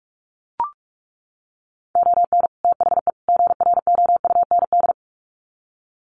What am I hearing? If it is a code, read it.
Morse code, heard as a keyed tone, about 44 wpm, 700 Hz. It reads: ON THE GROUND
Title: Sets of 3 Words 45wpm